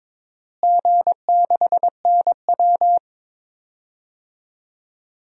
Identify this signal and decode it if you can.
Morse code, heard as a keyed tone, about 22 wpm, 700 Hz. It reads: G6NW
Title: DXCC-100 Call Signs - 4 Characters - Repeated 2x - 22wpm